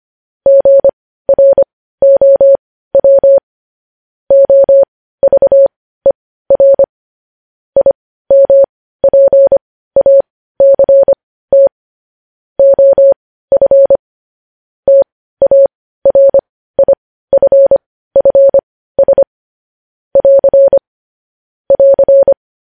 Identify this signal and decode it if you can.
Morse code, heard as a keyed tone, about 25 wpm, spaced as for 15 wpm, 550 Hz. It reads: GROW OVER IMPACT OF TARIFFS <AR> <AR>
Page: news headlines in morse code at 15 wpm